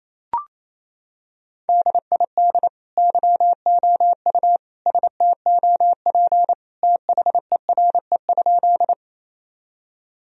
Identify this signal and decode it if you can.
Morse code, heard as a keyed tone, about 28 wpm, 700 Hz. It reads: DID YOU STOP THERE?
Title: Sentences from Top 1000 Words - Review Part 1 - 28wpm